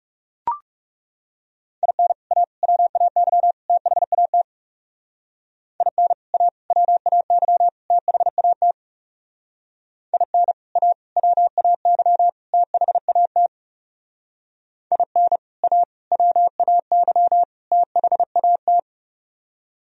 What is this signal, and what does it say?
Morse code, heard as a keyed tone, about 37 wpm, 700 Hz. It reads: IN A WAY THAT IN A WAY THAT IN A WAY THAT IN A WAY THAT
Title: Speed-Racing - Sets of 4 Words 30wpm